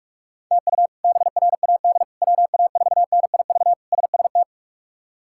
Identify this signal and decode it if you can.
Morse code, heard as a keyed tone, about 41 wpm, 700 Hz. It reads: TU BRAD WA4NIV SST
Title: K1USN SST QSO Practice - 45wpm